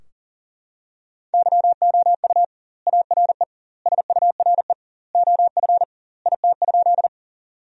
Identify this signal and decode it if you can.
Morse code, heard as a keyed tone, about 40 wpm, 700 Hz. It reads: YOU ARE SURE OF IT?
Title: 2x Word-Spacing - Sentences from Top 300 Words - 40wpm